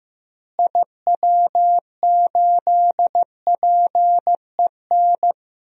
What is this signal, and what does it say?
Morse code, heard as a keyed tone, about 15 wpm, 700 Hz. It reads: IW8PEN
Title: DXCC-100 Call Signs - 6 Characters - Repeated 2x - 15wpm